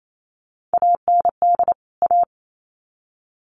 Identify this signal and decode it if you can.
Morse code, heard as a keyed tone, about 28 wpm, 700 Hz. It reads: AND A
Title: Sets of 2 Words 28wpm